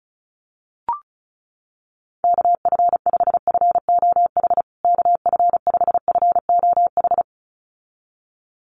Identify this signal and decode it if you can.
Morse code, heard as a keyed tone, about 34 wpm, 700 Hz. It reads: KF5FOH KF5FOH
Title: USA Call Signs Repeated 2x 35wpm